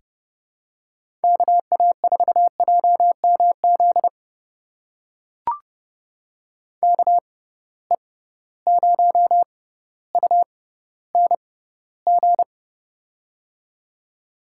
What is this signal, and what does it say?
Morse code, heard as a keyed tone, about 32 wpm, spaced as for 16 wpm, 700 Hz. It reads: KA4JMZ KE0UNG
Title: USA Call Signs - 6 Characters - 6x Character Spacing - 30wpm